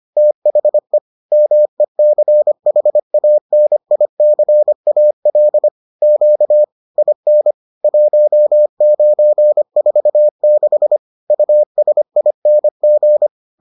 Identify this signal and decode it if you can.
Morse code, heard as a keyed tone, about 25 wpm, 600 Hz. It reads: THE MECHANICAL Q IN 1946 USING